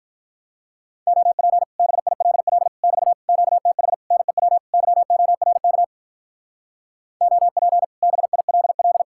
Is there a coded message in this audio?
Yes: Morse code from 1.1 to 9.0 s, heard as a keyed tone, about 44 wpm, 700 Hz. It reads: OP BILL <BT> QTH NEW YORK OP BILL